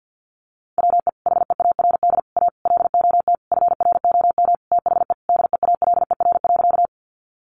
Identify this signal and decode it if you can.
Morse code, heard as a keyed tone, about 50 wpm, 700 Hz. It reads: WE HEARD A LOT FROM THE DEALER.
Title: Sentences from Top 1100 Words 50wpm